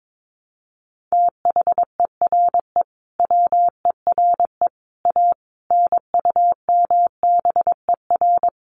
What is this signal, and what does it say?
Morse code, heard as a keyed tone, about 22 wpm, 700 Hz. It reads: THERE WERE A NUMBER
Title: Sets of 5 Words Encore2 22wpm